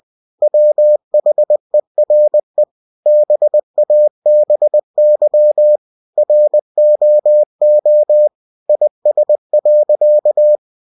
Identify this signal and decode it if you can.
Morse code, heard as a keyed tone, about 20 wpm, 600 Hz. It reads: WHERE BABY ROO IS.
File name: cw-books / winnie-the-pooh / winnie-20wpm-page-0061